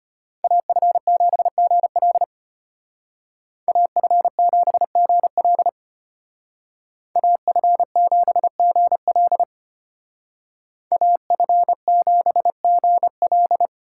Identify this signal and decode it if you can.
Morse code, heard as a keyed tone, about 32 wpm, 700 Hz. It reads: AF7GL AF7GL AF7GL AF7GL